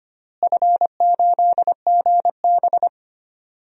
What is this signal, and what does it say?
Morse code, heard as a keyed tone, about 25 wpm, 700 Hz. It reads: F8GB
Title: DXCC-100 Call Signs - 4 Characters - Repeated 3x - 25wpm